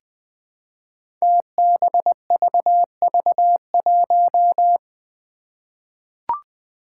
Morse code, heard as a keyed tone, about 20 wpm, 700 Hz. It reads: TBVV1
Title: Five Challenging Letter-Number Combinations 20wpm